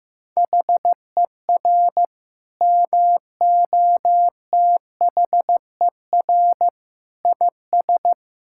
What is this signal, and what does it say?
Morse code, heard as a keyed tone, about 15 wpm, 700 Hz. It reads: HER MOTHER IS
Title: Sentences from Top 1000 Words - Review Part 5 - 15wpm